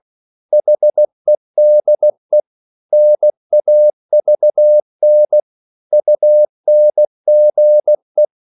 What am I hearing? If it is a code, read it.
Morse code, heard as a keyed tone, about 16 wpm, 600 Hz. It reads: HEDE NAVN UNGE
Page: Korte danske ord 16wpm